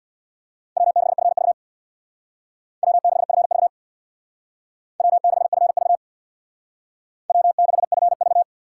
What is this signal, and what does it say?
Morse code, heard as a keyed tone, about 62 wpm, 700 Hz. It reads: W6FV W6FV W6FV W6FV
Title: Speed Racing - USA 1x2 Call Signs 50wpm